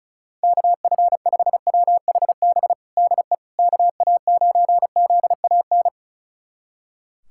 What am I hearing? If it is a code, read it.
Morse code, heard as a keyed tone, about 34 wpm, 700 Hz. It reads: KF5WHB DE KA9ZAN
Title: USA Callsign DE Callsign - 6 Characters - 35wpm